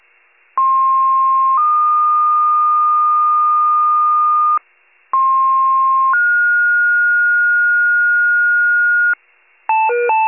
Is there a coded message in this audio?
Yes: two-tone page from 0.6 to 9.1 s, heard as a one-second tone, then a three-second tone, about 1060 then 1220 Hz, about 1060 then 1430 Hz, with noise.